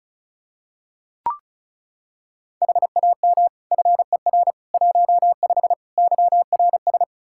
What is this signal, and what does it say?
Morse code, heard as a keyed tone, about 36 wpm, 700 Hz. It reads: HAM FER 15 YRS
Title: K1USN SST QSO Practice 35wpm